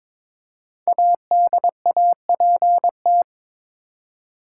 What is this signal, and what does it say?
Morse code, heard as a keyed tone, about 22 wpm, 700 Hz. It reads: ADAPT
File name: 5 Letter Words - First 3 Letters Given - 22wpm